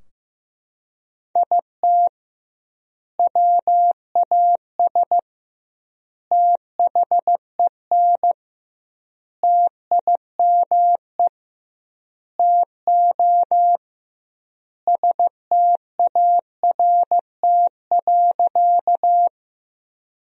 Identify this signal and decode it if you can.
Morse code, heard as a keyed tone, about 15 wpm, 700 Hz. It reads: IT WAS THEN TIME TO START.
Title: 2x Word-Spacing - Sentences from Top 300 Words - 15wpm